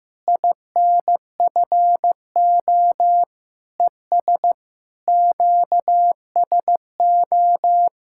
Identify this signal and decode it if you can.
Morse code, heard as a keyed tone, about 14 wpm, 700 Hz. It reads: INFO ES QSO
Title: Practice QSO - Exchange 4 - Ending - 15wpm